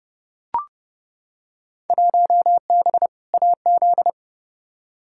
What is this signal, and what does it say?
Morse code, heard as a keyed tone, about 30 wpm, 700 Hz. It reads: 1B AZ
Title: Field Day Exchanges Encore - 30wpm